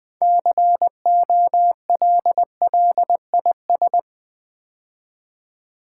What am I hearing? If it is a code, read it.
Morse code, heard as a keyed tone, about 20 wpm, 700 Hz. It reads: COLLIS
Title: Build A Word - 3 Letters At A Time - 9-Letter Words - 20wpm